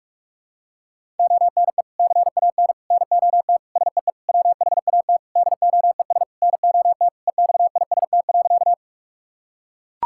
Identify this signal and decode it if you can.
Morse code, heard as a keyed tone, about 41 wpm, 700 Hz. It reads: ONE CAN NOT SEE WHAT DOES NOT EXIST.
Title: Sentences from Top 1000 Words - Review Part 5 - 45wpm